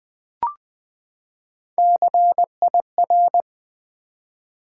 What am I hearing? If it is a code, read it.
Morse code, heard as a keyed tone, about 20 wpm, 700 Hz. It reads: CIR
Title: Build A Word - 3 Letters At A Time - 6-Letter Words - 20wpm